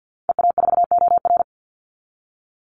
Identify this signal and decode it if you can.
Morse code, heard as a keyed tone, about 50 wpm, 700 Hz. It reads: EA4OR
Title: Call Signs Repeated 2x - 50wpm